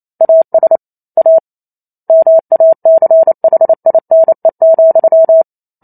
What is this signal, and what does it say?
Morse code, heard as a keyed tone, about 28 wpm, 650 Hz. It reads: AS A MACHINE,